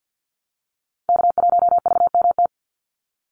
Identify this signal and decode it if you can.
Morse code, heard as a keyed tone, about 50 wpm, 700 Hz. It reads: K1VMT
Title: Call Signs Repeated 2x - 50wpm